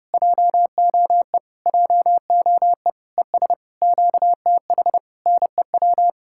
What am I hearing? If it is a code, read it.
Morse code, heard as a keyed tone, about 33 wpm, 700 Hz. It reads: JOE JOE ES QTH NEW